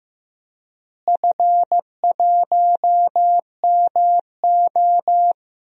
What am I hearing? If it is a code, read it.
Morse code, heard as a keyed tone, about 16 wpm, 700 Hz. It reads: F1MO
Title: DXCC-100 Call Signs - 4 Characters - Repeated 3x - 15wpm